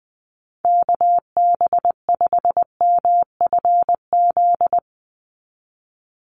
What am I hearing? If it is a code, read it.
Morse code, heard as a keyed tone, about 19 wpm, 700 Hz. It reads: KB5MFZ
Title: USA Call Signs 20wpm